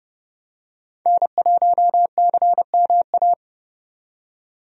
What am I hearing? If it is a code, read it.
Morse code, heard as a keyed tone, about 30 wpm, 700 Hz. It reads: N1CMA
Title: USA Call Signs - 5 Characters - 30wpm